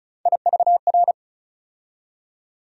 Morse code, heard as a keyed tone, about 36 wpm, 700 Hz. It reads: IVR
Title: USA Call Sign Postfixes - 3 Letters 35wpm